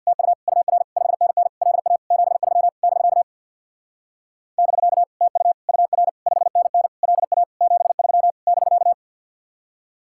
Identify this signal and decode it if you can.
Morse code, heard as a keyed tone, about 61 wpm, 700 Hz. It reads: TU UR 5NN LA 73 <BK> <BK> TU UR 5NN LA 73 <BK>
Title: Speed-Racing Common QSO Phrases - 50wpm